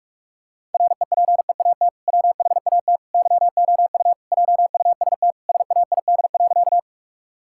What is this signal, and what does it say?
Morse code, heard as a keyed tone, about 41 wpm, 700 Hz. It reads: REPEAT WHAT YOU JUST SAID.
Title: Sentences from Top 1600 Words 45wpm